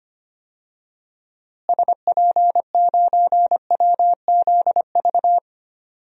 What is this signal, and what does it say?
Morse code, heard as a keyed tone, about 25 wpm, 700 Hz. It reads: SP9WZV
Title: DXCC-100 Call Signs - 6 Characters - Repeated 2x - 25wpm